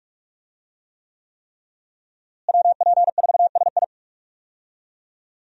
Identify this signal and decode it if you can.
Morse code, heard as a keyed tone, about 46 wpm, 700 Hz. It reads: WP4SI
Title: USA Call Signs - 5 Characters - Repeated 3x - 45wpm